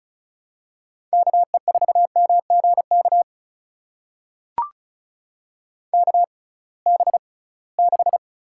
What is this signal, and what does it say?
Morse code, heard as a keyed tone, about 36 wpm, spaced as for 23 wpm, 700 Hz. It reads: KE4MGK KB6
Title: USA Call Signs - 6 Characters - 6x Character Spacing - 35wpm